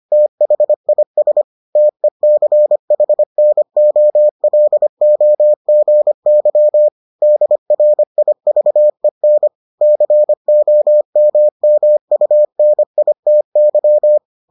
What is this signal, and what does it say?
Morse code, heard as a keyed tone, about 25 wpm, 600 Hz. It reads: THIS TECHNOLOGY DRIVEN COMMUNITY